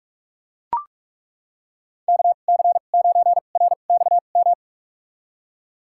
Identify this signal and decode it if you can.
Morse code, heard as a keyed tone, about 41 wpm, spaced as for 31 wpm, 700 Hz. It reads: KC9RXM
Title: USA Call Signs - 6 Characters - 2x Character Spacing - 45wpm